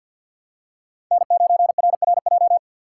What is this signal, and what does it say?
Morse code, heard as a keyed tone, about 51 wpm, 700 Hz. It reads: N9RRJ
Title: USA Call Signs - 3x Character Spacing - 50wpm